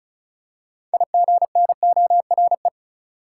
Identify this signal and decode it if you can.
Morse code, heard as a keyed tone, about 34 wpm, 700 Hz. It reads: IGNORE
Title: Six Letter Words - 35wpm